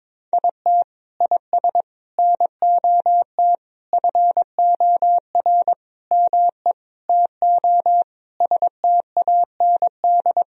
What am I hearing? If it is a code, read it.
Morse code, heard as a keyed tone, about 22 wpm, 700 Hz. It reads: IT IS NOT FOR ME TO STAND
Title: Sentences from Top 1000 Words - Review Part 1 - 22wpm